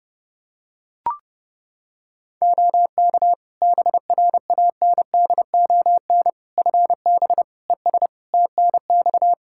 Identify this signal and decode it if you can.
Morse code, heard as a keyed tone, about 32 wpm, 700 Hz. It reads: OK BRANDON FB ES TNX
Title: K1USN SST QSO Practice 30wpm